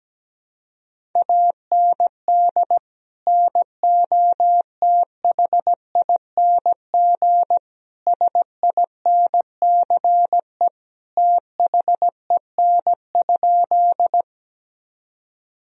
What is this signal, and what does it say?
Morse code, heard as a keyed tone, about 17 wpm, 700 Hz. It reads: AND NOTHING SINCE THEN?
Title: Sentences from Top 1000 Words - Review Part 1 - 17wpm